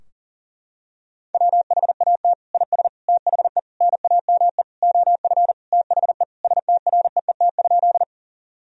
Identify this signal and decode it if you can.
Morse code, heard as a keyed tone, about 40 wpm, 700 Hz. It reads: WHAT IS THE NAME OF THE STREET?
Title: Sentences from Top 1000 Words - Review Part 3 - 40wpm